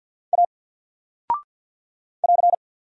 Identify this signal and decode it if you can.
Morse code, heard as a keyed tone, about 48 wpm, 700 Hz. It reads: A <AR>
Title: Single Character (Letter:Number:Punctuation:Prosign) 50wpm